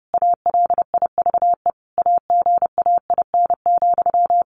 Morse code, heard as a keyed tone, about 30 wpm, 700 Hz. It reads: ALIVE AGAIN,
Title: A Sound Of Thunder 30wpm